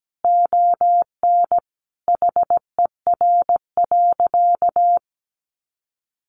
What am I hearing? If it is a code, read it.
Morse code, heard as a keyed tone, about 17 wpm, 700 Hz. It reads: ON HER.
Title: Sentences from Top 1200 Words 17wpm